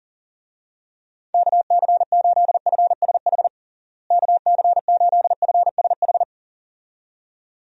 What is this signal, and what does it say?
Morse code, heard as a keyed tone, about 40 wpm, 700 Hz. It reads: KC8FSH KC8FSH
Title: USA Call Signs - 6 Characters - Repeated 2x - 40wpm